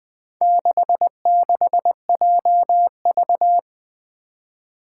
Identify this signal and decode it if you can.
Morse code, heard as a keyed tone, about 20 wpm, 700 Hz. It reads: 66JV
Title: Four Challenging Letter-Number Combinations 20wpm